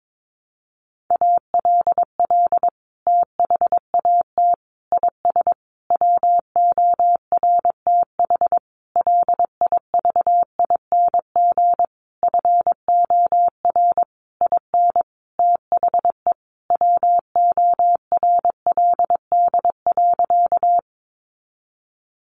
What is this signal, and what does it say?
Morse code, heard as a keyed tone, about 22 wpm, 700 Hz. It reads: ALL THAT IS WORTH LIVING FOR IN THE WORLD.
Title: Sentences from Top 1100 Words 22wpm